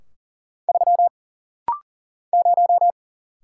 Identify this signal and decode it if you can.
Morse code, heard as a keyed tone, about 40 wpm, 700 Hz. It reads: 3 0
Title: Single Letter-Number - Rapid-Fire 40wpm